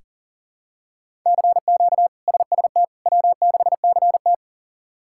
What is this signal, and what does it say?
Morse code, heard as a keyed tone, about 40 wpm, 700 Hz. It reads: CQ SST W6CT
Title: K1USN SST QSO Practice - 40wpm